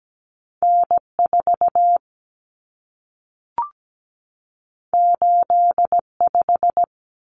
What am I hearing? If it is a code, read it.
Morse code, heard as a keyed tone, about 17 wpm, 700 Hz. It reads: N4 85
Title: Common Cut Numbers - 2-Digit 17wpm